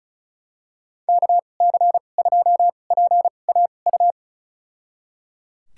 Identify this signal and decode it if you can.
Morse code, heard as a keyed tone, about 34 wpm, spaced as for 25 wpm, 700 Hz. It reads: KC2PAU
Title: USA Call Signs - 6 Characters - 2x Character Spacing - 35wpm